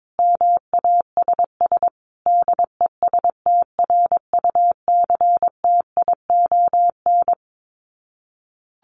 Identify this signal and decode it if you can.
Morse code, heard as a keyed tone, about 22 wpm, 700 Hz. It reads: MASS DESTRUCTION
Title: Sets of 5 Words Encore 22wpm